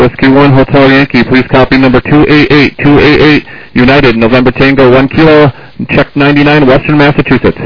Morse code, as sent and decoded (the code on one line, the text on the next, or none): none